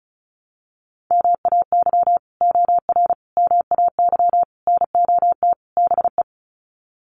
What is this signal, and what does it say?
Morse code, heard as a keyed tone, about 34 wpm, 700 Hz. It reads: MAY OR MAY NOT BE
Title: Sets of 5 Words 35wpm